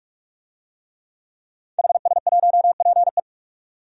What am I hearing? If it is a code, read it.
Morse code, heard as a keyed tone, about 46 wpm, 700 Hz. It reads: HS1PE
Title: DXCC-100 Call Signs - 5 Characters - 45wpm